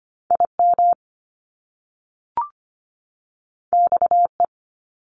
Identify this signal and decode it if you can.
Morse code, heard as a keyed tone, about 25 wpm, 700 Hz. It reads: IM XE
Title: Top Two Letter Combinations in Words 25wpm